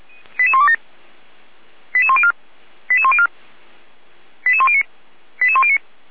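Five-tone selective call (ZVEI), heard as quick runs of five tones, tones of about 70 ms. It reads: 88127 88104 88104 88109 88109
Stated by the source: Five-tone ZVEI selective call with code 88064, received by SDR.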